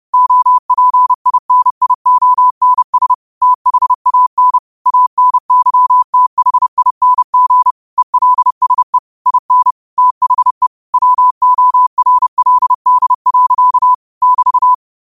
Morse code, heard as a keyed tone, about 30 wpm, 1000 Hz. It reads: OPINIONS THAN ANYTHING ELSE IN THE WORLD. <BT>
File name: qotdfiles / 30wpm